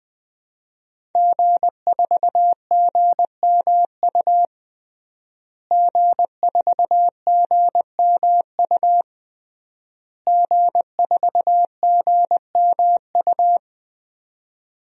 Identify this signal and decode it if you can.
Morse code, heard as a keyed tone, about 19 wpm, 700 Hz. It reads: G4GMU G4GMU G4GMU
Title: DXCC-100 Call Signs - 5 Characters - Repeated 3x - 20wpm